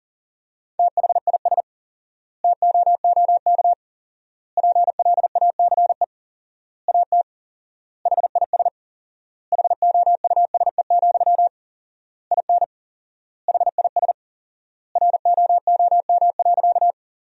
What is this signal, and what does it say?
Morse code, heard as a keyed tone, about 40 wpm, 700 Hz. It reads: THIS TOOK PLACE AT HIS HOUSE, IN HIS ROOM.
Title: 4x Word-Spacing - Sentences from Top 300 Words - 40wpm